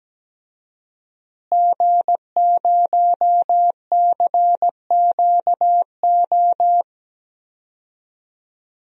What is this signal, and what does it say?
Morse code, heard as a keyed tone, about 17 wpm, 700 Hz. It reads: G0CQO
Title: DXCC-100 Call Signs - 5 Characters - Repeated 2x - 17wpm